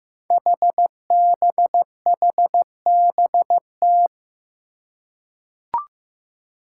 Morse code, heard as a keyed tone, about 15 wpm, 700 Hz. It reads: HBHBT
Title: Five Challenging Letter-Number Combinations 15wpm